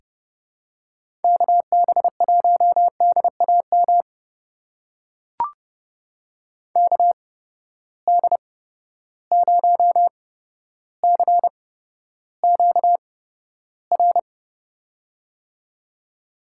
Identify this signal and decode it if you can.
Morse code, heard as a keyed tone, about 30 wpm, spaced as for 13 wpm, 700 Hz. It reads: KB1DAM KD0CQR
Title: USA Call Signs - 6 Characters - 8x Character Spacing - 30wpm